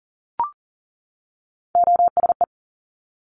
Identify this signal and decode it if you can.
Morse code, heard as a keyed tone, about 38 wpm, 700 Hz. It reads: OSE
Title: Top Three Letter Combinations in Words 40wpm